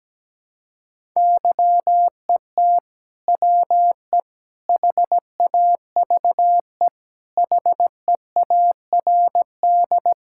Morse code, heard as a keyed tone, about 17 wpm, 700 Hz. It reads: YET WE HAVE HEARD
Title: Sentences from Top 1000 Words - Review Part 5 - 17wpm